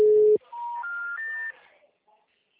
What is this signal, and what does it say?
Special information tone.